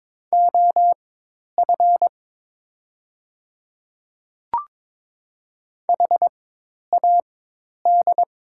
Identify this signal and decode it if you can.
Morse code, heard as a keyed tone, about 22 wpm, spaced as for 10 wpm, 690 Hz. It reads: OF HAD
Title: 4x Character Spacing - Top 200 Words - 22wpm